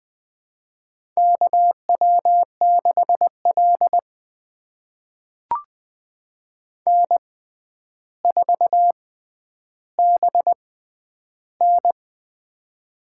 Morse code, heard as a keyed tone, about 20 wpm, spaced as for 10 wpm, 700 Hz. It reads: KW6L N4BN
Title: USA Call Signs - 4 Characters - 6x Character Spacing - 20wpm